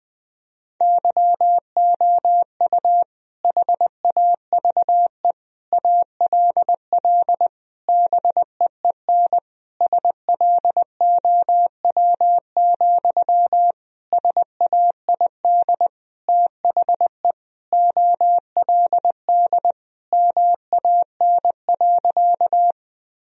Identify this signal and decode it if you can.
Morse code, heard as a keyed tone, about 20 wpm, 700 Hz. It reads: YOU HAVE ALL BEEN SLOW, SAID THE OLD MAN.
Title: Sentences from Top 1000 Words - Review Part 2 - 20wpm